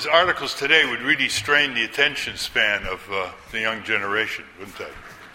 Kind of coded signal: Emergency Alert System (SAME) tones